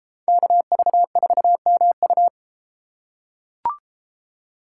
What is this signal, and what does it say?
Morse code, heard as a keyed tone, about 32 wpm, 700 Hz. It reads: KV4MU